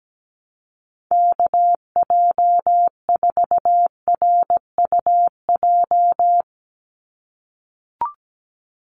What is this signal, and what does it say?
Morse code, heard as a keyed tone, about 17 wpm, 700 Hz. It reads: KJ4RUJ
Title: USA Call Signs 17wpm